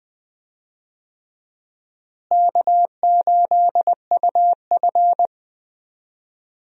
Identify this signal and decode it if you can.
Morse code, heard as a keyed tone, about 20 wpm, 700 Hz. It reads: K8UF
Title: USA Call Signs - 4 Characters - Repeated 2x - 20wpm